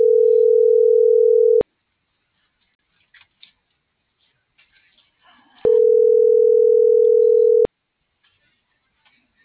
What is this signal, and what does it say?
Ringback tone (North America), 440 + 480 Hz.